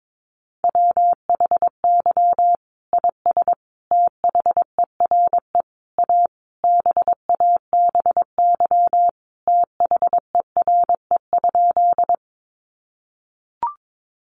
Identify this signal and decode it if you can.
Morse code, heard as a keyed tone, about 22 wpm, 700 Hz. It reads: WHY IS THERE A BABY THERE?
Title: Sentences from Top 1300 Words 22wpm